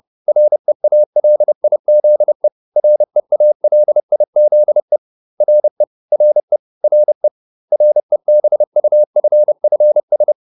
Morse code, heard as a keyed tone, about 30 wpm, 600 Hz. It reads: REALIZE REALIZE RE RE RE REBUFFS